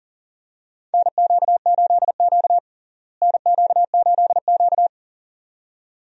Morse code, heard as a keyed tone, about 40 wpm, 700 Hz. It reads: NQ8Q NQ8Q
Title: USA Call Signs - 4 Characters - Repeated 2x - 40wpm